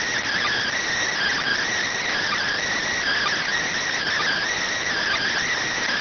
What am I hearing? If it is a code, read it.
Emergency Alert System (SAME) tones (header bursts), heard as harsh buzzing bursts. It is heard from 1.1 to 5.5 s.